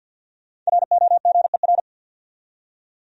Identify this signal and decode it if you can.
Morse code, heard as a keyed tone, about 47 wpm, 700 Hz. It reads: ROGER
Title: USA Common Names - 50wpm